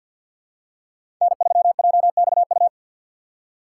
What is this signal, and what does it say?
Morse code, heard as a keyed tone, about 53 wpm, 700 Hz. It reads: N3JXU
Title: USA Call Signs - 5 Characters - 6x Character Spacing - 50wpm